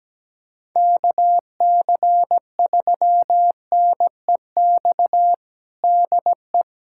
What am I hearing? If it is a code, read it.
Morse code, heard as a keyed tone, about 17 wpm, 700 Hz. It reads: KC3NEX DE
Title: USA Callsign DE Callsign - 6 Characters - 17wpm